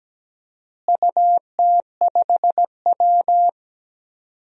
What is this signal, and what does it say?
Morse code, heard as a keyed tone, about 17 wpm, 700 Hz. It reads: UT5W
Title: DXCC-100 Call Signs - 4 Characters - Repeated 2x - 17wpm